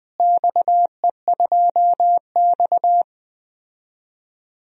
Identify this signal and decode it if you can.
Morse code, heard as a keyed tone, about 21 wpm, 700 Hz. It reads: XE2X
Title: DXCC-100 Call Signs - 4 Characters - Repeated 2x - 20wpm